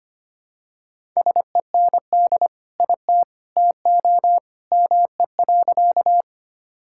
Morse code, heard as a keyed tone, about 25 wpm, 700 Hz. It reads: SEND IT TO ME.
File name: practice / 25wpm / Sentences with 2-Word Phrases - 25wpm